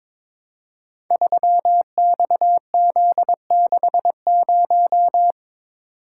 Morse code, heard as a keyed tone, about 22 wpm, 700 Hz. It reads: 3XZ60
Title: Code Groups - 5 Letter:Numbers - 22wpm